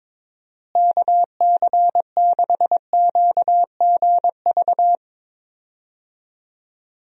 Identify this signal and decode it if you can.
Morse code, heard as a keyed tone, about 22 wpm, 700 Hz. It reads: KC6QGV
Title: USA Call Signs - 6 Characters - Repeated 3x - 22wpm